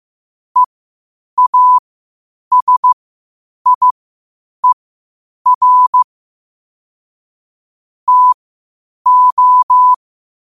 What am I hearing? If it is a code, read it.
Morse code, heard as a keyed tone, about 15 wpm, spaced as for 8 wpm, 1000 Hz. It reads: EASIER TO
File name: qotdfiles / 8wpm